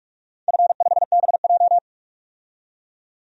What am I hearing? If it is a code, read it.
Morse code, heard as a keyed tone, about 44 wpm, 700 Hz. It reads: F5BJ
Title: DXCC-100 Call Signs - 4 Characters - 45wpm